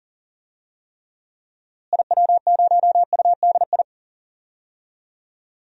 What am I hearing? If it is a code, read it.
Morse code, heard as a keyed tone, about 40 wpm, 700 Hz. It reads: IW0UDI
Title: DXCC-100 Call Signs - 6 Characters - Repeated 2x - 40wpm